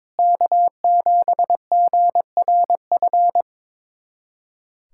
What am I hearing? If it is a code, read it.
Morse code, heard as a keyed tone, about 21 wpm, 700 Hz. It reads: K7GRF
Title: USA Callsign DE Callsign - 5 Characters - 22wpm